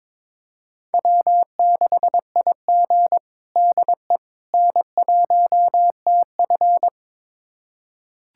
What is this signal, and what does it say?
Morse code, heard as a keyed tone, about 21 wpm, 700 Hz. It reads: W6IG DE N1TF
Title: USA Callsign DE Callsign - 4 Characters - 22wpm